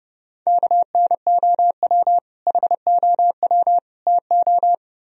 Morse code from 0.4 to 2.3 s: KNOW

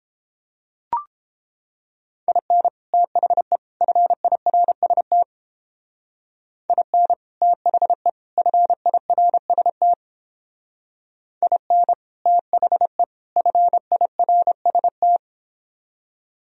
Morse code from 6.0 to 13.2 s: IN THE FIRST IN THE

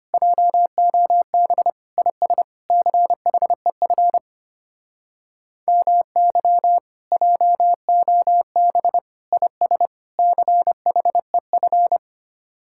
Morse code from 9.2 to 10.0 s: IS

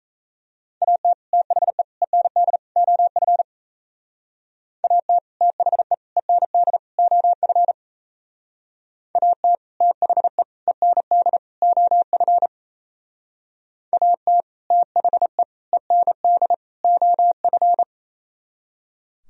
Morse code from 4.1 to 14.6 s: AT THE END OF AT THE END OF AT

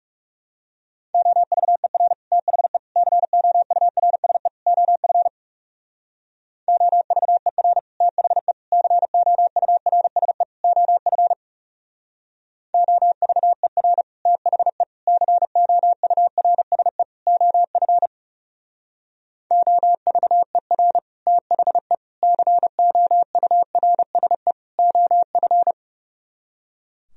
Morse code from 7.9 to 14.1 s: THE COURSE OF OVER

